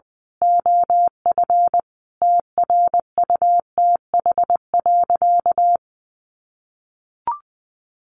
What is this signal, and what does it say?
Morse code, heard as a keyed tone, about 20 wpm, 700 Hz. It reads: OF TRUTH.
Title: Sentences from Top 1100 Words 20wpm